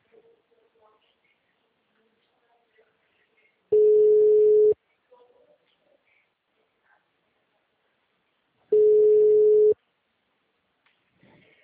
Ringback tone (Europe, 425 Hz).
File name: out- -2058- - - .42900